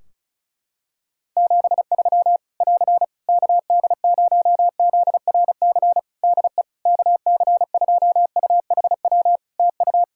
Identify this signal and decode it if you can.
Morse code, heard as a keyed tone, about 34 wpm, 700 Hz. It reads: 73 <AR> KD0ZRC DE KC2UHW TU
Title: Practice QSO - Exchange 4 - Ending - 35wpm